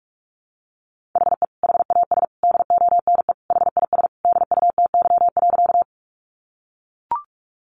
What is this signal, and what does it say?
Morse code, heard as a keyed tone, about 44 wpm, 700 Hz. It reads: HE HAS DONE HIS DUTY.
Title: Sentences from Top 1500 Words 45wpm